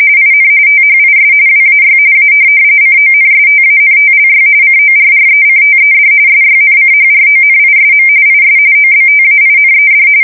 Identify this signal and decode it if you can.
RTTY, about 45.45 baud. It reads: IN THIS WAY A CONSTANTLY CORRECTED SYNCHRONISM IS OBTAINED
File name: fsk170 / printing-telegraph